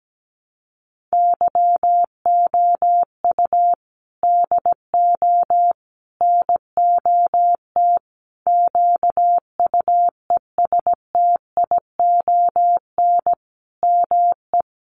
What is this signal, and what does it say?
Morse code, heard as a keyed tone, about 17 wpm, 700 Hz. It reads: YOU DO NOT QUESTION ME
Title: Sentences from Top 1100 Words 17wpm